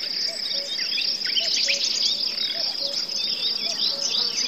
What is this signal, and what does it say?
There is no code.